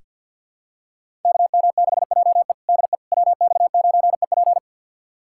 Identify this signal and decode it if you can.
Morse code, heard as a keyed tone, about 48 wpm, 700 Hz. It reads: KM6JE DE WX9EP
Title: USA Callsign DE Callsign 50wpm